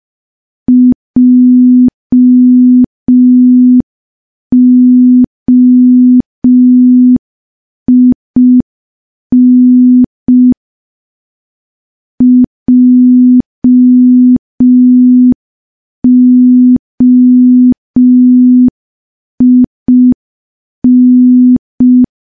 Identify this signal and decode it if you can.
Morse code, heard as a keyed tone, about 5 wpm, 250 Hz. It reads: JOIN JOIN
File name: morse / morse-training-05WPM